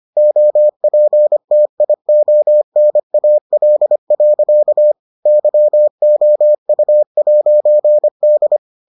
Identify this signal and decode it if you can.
Morse code, heard as a keyed tone, about 25 wpm, 600 Hz. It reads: OPTIONAL. YOU'D